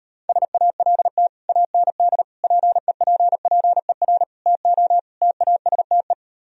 Morse code, heard as a keyed tone, about 38 wpm, 700 Hz. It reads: SALT AND PEPPER TO TASTE